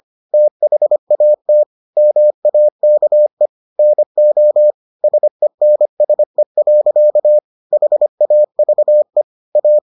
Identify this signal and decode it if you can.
Morse code, heard as a keyed tone, about 25 wpm, 600 Hz. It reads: THAT MAKE NO SENSE. HAVE A